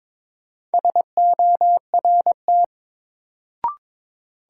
Morse code, heard as a keyed tone, about 22 wpm, 700 Hz. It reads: SORT
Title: Sentences from Top 1300 Words - New Word Drill Alt - 22wpm